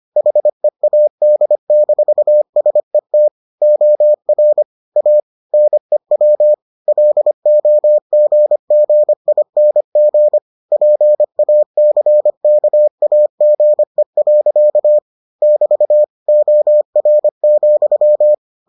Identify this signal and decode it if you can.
Morse code, heard as a keyed tone, about 25 wpm, 600 Hz. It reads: HEAD-SET OR A NEW LOGGING PACKAGE. <BT> OR,